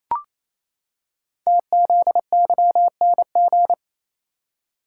Morse code, heard as a keyed tone, about 29 wpm, 700 Hz. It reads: TZYNG
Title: Code Groups - 5 Letter:Numbers - 28wpm